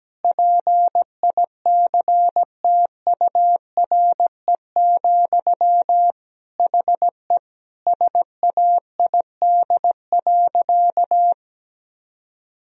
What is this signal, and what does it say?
Morse code, heard as a keyed tone, about 17 wpm, 700 Hz. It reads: PICTURE, HE SAID.
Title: Sentences from Top 1000 Words - Review Part 1 - 17wpm